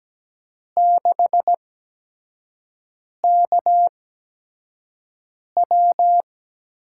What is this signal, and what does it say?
Morse code, heard as a keyed tone, about 17 wpm, spaced as for 5 wpm, 700 Hz. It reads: 6KW
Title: USA Call Signs - 5 Characters - 8x Character Spacing - 17wpm